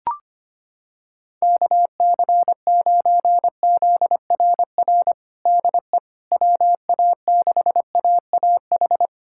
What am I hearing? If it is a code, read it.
Morse code, heard as a keyed tone, about 25 wpm, 700 Hz. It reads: KC9ZRR DE WA6AAH
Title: USA Callsign DE Callsign 25wpm